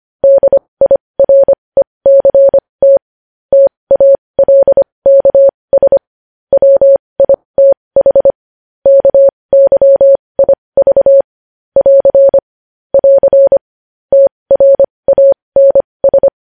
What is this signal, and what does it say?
Morse code, heard as a keyed tone, about 25 wpm, spaced as for 20 wpm, 550 Hz. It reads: DIRECT TALKS WITH KYIV <AR> <AR> TRANS
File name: podcast / News headlines-1x-25x20